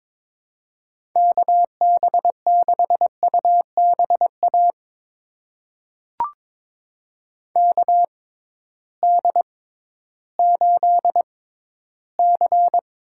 Morse code, heard as a keyed tone, about 22 wpm, spaced as for 13 wpm, 700 Hz. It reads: KB6UBA KD8C
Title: USA Call Signs - 6 Characters - 6x Character Spacing - 22wpm